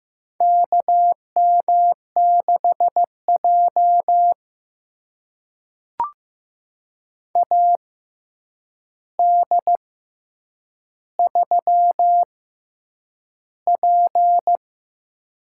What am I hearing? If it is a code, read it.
Morse code, heard as a keyed tone, about 15 wpm, spaced as for 8 wpm, 700 Hz. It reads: KM6J AD3P
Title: USA Call Signs - 4 Characters - 6x Character Spacing - 15wpm